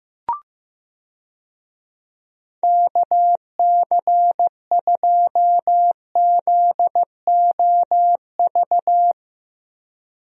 Morse code, heard as a keyed tone, about 15 wpm, 700 Hz. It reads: KC2ZOV
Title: USA Call Signs - 6 Characters - Repeated 2x - 15wpm